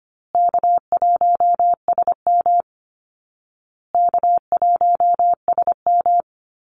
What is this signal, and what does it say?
Morse code, heard as a keyed tone, about 25 wpm, 700 Hz. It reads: K1SM K1SM
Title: Easy Call Signs Repeated 3x - 25wpm